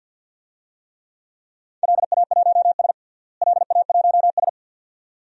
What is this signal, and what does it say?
Morse code, heard as a keyed tone, about 47 wpm, 700 Hz. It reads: LA1S LA1S
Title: DXCC-100 Call Signs - 4 Characters - Repeated 2x - 50wpm